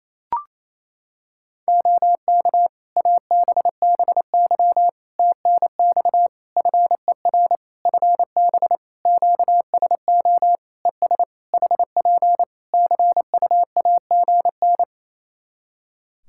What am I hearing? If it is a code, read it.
Morse code, heard as a keyed tone, about 28 wpm, 700 Hz. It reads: OK ABBY TNX FER FB QSO ES HP CUAGN
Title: Practice QSO - Exchange 4 - Ending - 28wpm